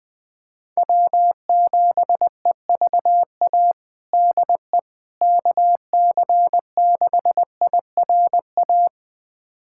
Morse code, heard as a keyed tone, about 20 wpm, 700 Hz. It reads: W7EVA DE KC6IRA
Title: USA Callsign DE Callsign 20wpm